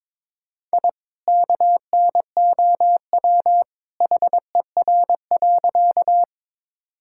Morse code, heard as a keyed tone, about 22 wpm, 700 Hz. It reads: I KNOW HER.
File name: practice / 22wpm / Sentences with 2-Word Phrases - 22wpm